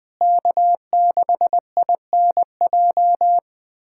Morse code, heard as a keyed tone, about 20 wpm, 700 Hz. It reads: K6INJ
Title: USA Callsign DE Callsign - 5 Characters - 20wpm